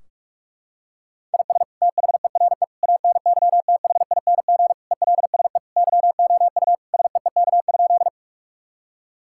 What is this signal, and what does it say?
Morse code, heard as a keyed tone, about 41 wpm, 700 Hz. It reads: IS THERE ANYTHING ELSE YOU SEEK?
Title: Sentences from Top 1600 Words 45wpm